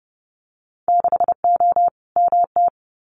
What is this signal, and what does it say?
Morse code, heard as a keyed tone, about 30 wpm, 700 Hz. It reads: 6O MT
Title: Winter Field Day Exchanges Encore 30wpm